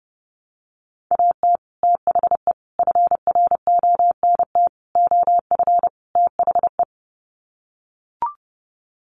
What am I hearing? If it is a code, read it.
Morse code, heard as a keyed tone, about 29 wpm, 700 Hz. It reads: AT THE FRONT OF THE
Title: Sets of 5 Words 30wpm